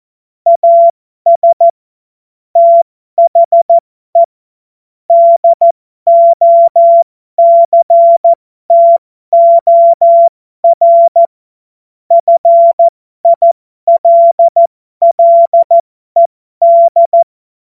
Morse code, heard as a keyed tone, about 14 wpm, spaced as for 12 wpm, 680 Hz. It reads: AS THE DOCTOR FILLED